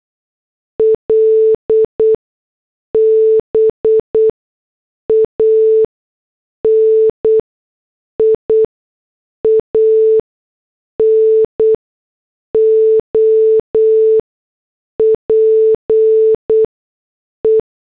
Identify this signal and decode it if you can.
Morse code, heard as a keyed tone, about 8 wpm, spaced as for 6 wpm, 440 Hz. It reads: LBANIAN OPE